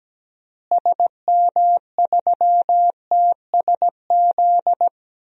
Morse code, heard as a keyed tone, about 17 wpm, 700 Hz. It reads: SM3TSZ
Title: DXCC-100 Call Signs - 6 Characters - Repeated 2x - 17wpm